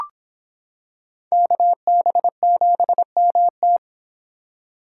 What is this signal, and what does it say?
Morse code, heard as a keyed tone, about 25 wpm, 700 Hz. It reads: KB7MT